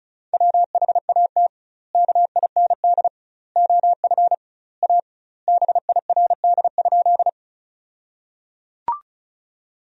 Morse code, heard as a keyed tone, about 36 wpm, 700 Hz. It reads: WHAT KIND OF A BIRD?
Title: 2x Word-Spacing - Sentences from Top 400 Words - 35wpm